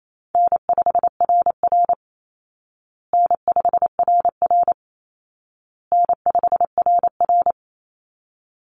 Morse code, heard as a keyed tone, about 28 wpm, 700 Hz. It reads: N5RR N5RR N5RR
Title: Easy Call Signs Repeated 3x - 28wpm